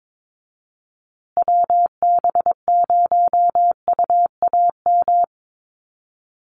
Morse code, heard as a keyed tone, about 22 wpm, 700 Hz. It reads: WB0UAM
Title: USA Call Signs Repeated 2x 22wpm